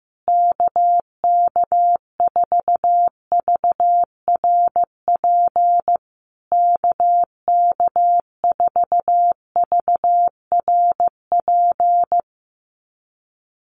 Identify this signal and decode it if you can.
Morse code, heard as a keyed tone, about 15 wpm, 700 Hz. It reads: KK4VRP KK4VRP
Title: USA Call Signs Repeated 3x 15wpm